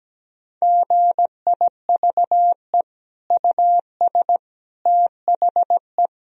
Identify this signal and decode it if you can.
Morse code, heard as a keyed tone, about 17 wpm, 700 Hz. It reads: GIVE US THE
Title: Sentences from Top 1000 Words - Review Part 4 - 17wpm